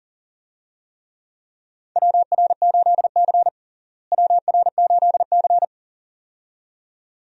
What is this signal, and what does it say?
Morse code, heard as a keyed tone, about 37 wpm, 700 Hz. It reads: WR8C WR8C
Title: USA Call Signs - 4 Characters - Repeated 2x - 40wpm